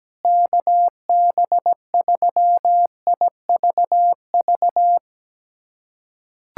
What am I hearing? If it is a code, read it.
Morse code, heard as a keyed tone, about 17 wpm, 700 Hz. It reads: KB3IVV
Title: USA Callsign DE Callsign - 6 Characters - 17wpm